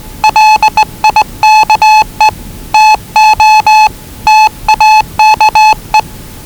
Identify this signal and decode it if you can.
Morse code, heard as a keyed tone, about 20 wpm, 840 Hz. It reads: LIKE TO TAKE
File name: WellingtonRadioZlwCloseDownOnHfAndMf / 01-Zlw 4277khz 417.5 qrt until 0925z